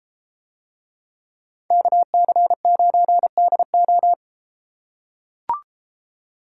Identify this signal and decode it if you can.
Morse code, heard as a keyed tone, about 33 wpm, 700 Hz. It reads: KC9DO